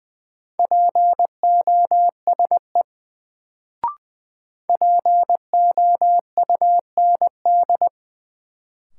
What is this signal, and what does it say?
Morse code, heard as a keyed tone, about 21 wpm, 700 Hz. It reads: POSE POUND
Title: Sentences from Top 1000 Words - Review Part 2 - New Word Drill - 20wpm